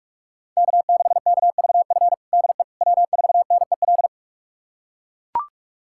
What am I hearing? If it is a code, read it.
Morse code, heard as a keyed tone, about 41 wpm, 700 Hz. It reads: K6KVF DE W4NEL
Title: USA Callsign DE Callsign - 5 Characters - 45wpm